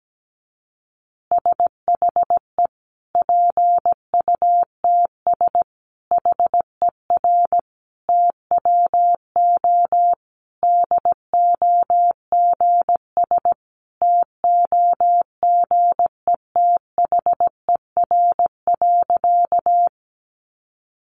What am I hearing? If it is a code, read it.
Morse code, heard as a keyed tone, about 17 wpm, 700 Hz. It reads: SHE PUTS HER TWO DOGS TOGETHER.
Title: Easy Phrases 17wpm